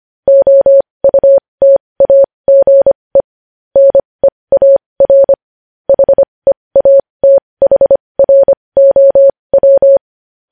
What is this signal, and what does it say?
Morse code, heard as a keyed tone, about 25 wpm, spaced as for 20 wpm, 550 Hz. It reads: OUTAGE NEAR HEATHROW
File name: podcast / News headlines-1x-25x20